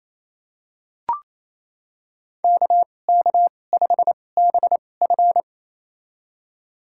Morse code, heard as a keyed tone, about 28 wpm, spaced as for 20 wpm, 700 Hz. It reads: KK5BF
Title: USA Call Signs - 5 Characters - 2x Character Spacing - 28wpm